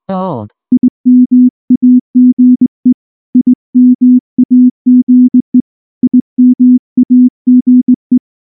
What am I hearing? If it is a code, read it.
Morse code, heard as a keyed tone, about 18 wpm, 250 Hz. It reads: IMAGE IMAGE IMAGE